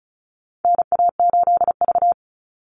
Morse code, heard as a keyed tone, about 34 wpm, 700 Hz. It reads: NA8V
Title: Call Signs - 35wpm